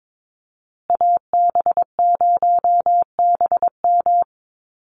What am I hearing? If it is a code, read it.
Morse code, heard as a keyed tone, about 22 wpm, 700 Hz. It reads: AB0BM
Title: USA Call Signs Repeated 2x 22wpm